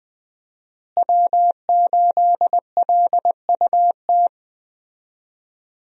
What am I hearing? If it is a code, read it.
Morse code, heard as a keyed tone, about 20 wpm, 700 Hz. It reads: W8LUT
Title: USA Call Signs - 5 Characters - Repeated 3x - 20wpm